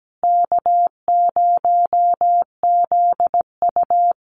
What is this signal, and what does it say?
Morse code, heard as a keyed tone, about 17 wpm, 700 Hz. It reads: K0ZU
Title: Easy Call Signs Repeated 3x - 17wpm